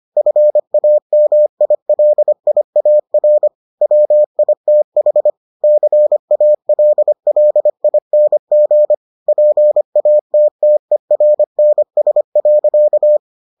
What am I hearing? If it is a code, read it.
Morse code, heard as a keyed tone, about 25 wpm, 600 Hz. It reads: FAMILIAR WITH CALLING PATTERNS.